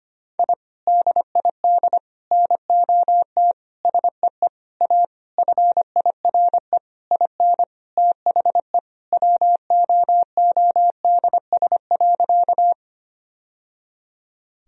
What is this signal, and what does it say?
Morse code, heard as a keyed tone, about 25 wpm, 700 Hz. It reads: I DID NOT SEE A FIRE IN THE WOODS.
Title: Sentences from Top 1000 Words - Review Part 2 - 25wpm